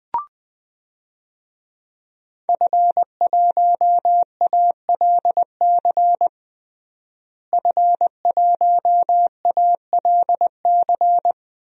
Morse code, heard as a keyed tone, about 20 wpm, 700 Hz. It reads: F1ALC F1ALC
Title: DXCC-100 Call Signs - 5 Characters - Repeated 3x - 20wpm